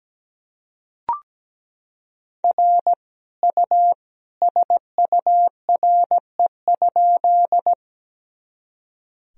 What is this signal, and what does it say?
Morse code, heard as a keyed tone, about 17 wpm, 700 Hz. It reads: R U SURE?
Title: Sentences from Top 300 Words and Abbreviations 17wpm